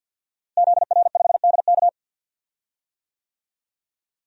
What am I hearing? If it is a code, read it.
Morse code, heard as a keyed tone, about 52 wpm, 700 Hz. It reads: ZR5DK
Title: DXCC-100 Call Signs - 5 Characters - Repeated 2x - 50wpm